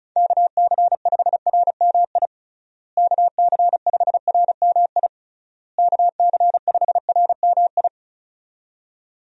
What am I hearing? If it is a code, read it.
Morse code, heard as a keyed tone, about 36 wpm, 700 Hz. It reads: KC5RMI KC5RMI KC5RMI
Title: USA Call Signs - 6 Characters - Repeated 3x - 35wpm